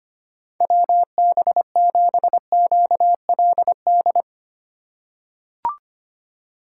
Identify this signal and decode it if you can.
Morse code, heard as a keyed tone, about 25 wpm, 700 Hz. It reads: WB7QLD
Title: USA Call Signs - 6 Characters - 2x Character Spacing - 25wpm